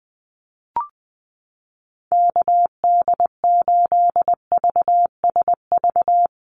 Morse code, heard as a keyed tone, about 20 wpm, 700 Hz. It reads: KD8VSV
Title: USA Call Signs Repeated 3x 20wpm